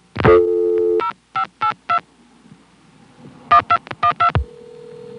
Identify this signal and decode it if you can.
DTMF, heard as telephone touch tones, tones of about 90 ms. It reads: 0582 1212